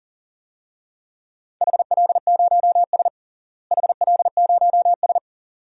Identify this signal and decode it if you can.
Morse code, heard as a keyed tone, about 40 wpm, 700 Hz. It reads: HL0S HL0S
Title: DXCC-100 Call Signs - 4 Characters - Repeated 2x - 40wpm